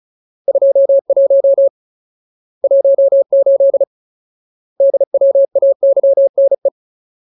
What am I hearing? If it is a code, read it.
Morse code, heard as a keyed tone, about 34 wpm, 550 Hz. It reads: 21 18 DWAYNE
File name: Ultimate Word List / 35WPM Ultimate